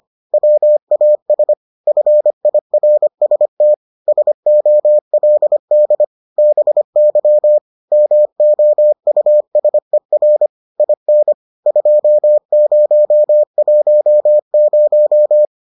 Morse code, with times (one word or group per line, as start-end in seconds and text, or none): 0.3-1.5 WAS
1.9-3.7 FIRST
4.1-6.0 SOLD
6.4-7.6 BY
7.9-10.5 MOUSER
10.8-11.3 IN
11.7-15.4 2010